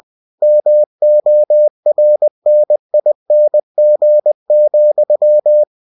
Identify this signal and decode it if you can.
Morse code, heard as a keyed tone, about 20 wpm, 600 Hz. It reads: MORNING,
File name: cw-books / winnie-the-pooh / winnie-20wpm-page-0004